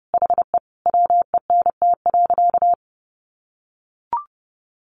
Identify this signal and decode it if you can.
Morse code, heard as a keyed tone, about 30 wpm, 700 Hz. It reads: HE WENT.
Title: Sentences from Top 600 Words 30wpm